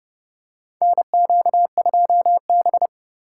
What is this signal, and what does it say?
Morse code, heard as a keyed tone, about 32 wpm, 700 Hz. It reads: NQ2B
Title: USA Call Signs - 4 Characters - 3x Character Spacing - 30wpm